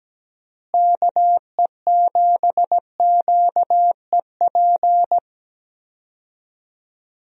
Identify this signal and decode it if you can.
Morse code, heard as a keyed tone, about 17 wpm, 700 Hz. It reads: KE7QEP
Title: USA Call Signs - 6 Characters - Repeated 3x - 17wpm